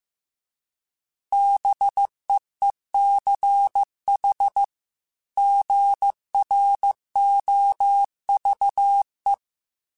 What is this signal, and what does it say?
Morse code, heard as a keyed tone, about 15 wpm, 800 Hz. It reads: BEECH GROVE
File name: fromSMs / FromSMs mp3s / Cities@13